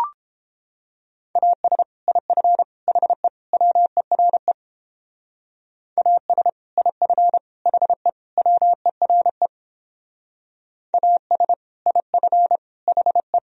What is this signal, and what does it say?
Morse code, heard as a keyed tone, about 29 wpm, 700 Hz. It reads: AS IF HE WERE AS IF HE WERE AS IF HE